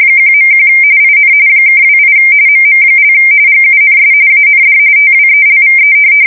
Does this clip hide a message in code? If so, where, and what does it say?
Yes: RTTY from 0.2 to 5.9 s, about 45.45 baud. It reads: 1.1 IN (28 MM) GUNS AND TWELVE